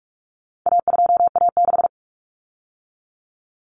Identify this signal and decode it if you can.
Morse code, heard as a keyed tone, about 44 wpm, 700 Hz. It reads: A2A6
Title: Common Cut Numbers - 4-Digit 45wpm